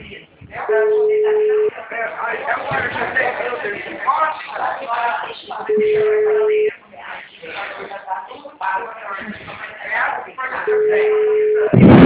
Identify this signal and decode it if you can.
Ringback tone (Europe, 425 Hz), with speech.